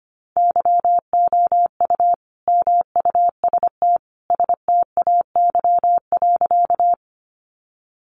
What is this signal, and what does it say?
Morse code, heard as a keyed tone, about 25 wpm, 700 Hz. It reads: YOU MUST STAY.
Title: Sentences from Top 500 Words 25wpm